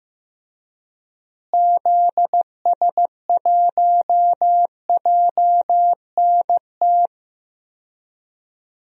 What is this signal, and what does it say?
Morse code, heard as a keyed tone, about 15 wpm, 700 Hz. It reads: ZS1JNT
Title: DXCC-100 Call Signs - 6 Characters - Repeated 3x - 15wpm